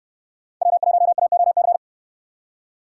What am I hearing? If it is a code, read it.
Morse code, heard as a keyed tone, about 68 wpm, 700 Hz. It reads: W1AOC